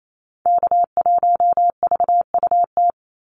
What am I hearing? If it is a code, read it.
Morse code, heard as a keyed tone, about 28 wpm, 700 Hz. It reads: K1VUT
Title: Call Signs Repeated 2x - 28wpm